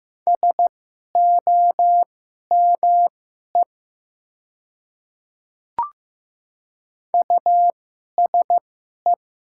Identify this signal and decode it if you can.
Morse code, heard as a keyed tone, about 15 wpm, spaced as for 11 wpm, 700 Hz. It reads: SOME USE
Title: 2x Character Spacing - Top 100 Words - 15wpm